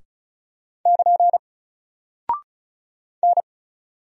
Morse code, heard as a keyed tone, about 36 wpm, 700 Hz. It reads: <KN> N
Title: Single Character (Letter:Number:Punctuation:Prosign) Alt 35wpm